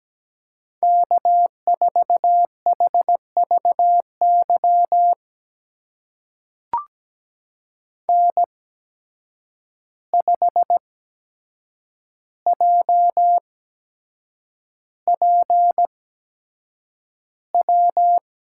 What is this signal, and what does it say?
Morse code, heard as a keyed tone, about 17 wpm, spaced as for 7 wpm, 700 Hz. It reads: K4HVY N5JPW
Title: USA Call Signs - 5 Characters - 8x Character Spacing - 17wpm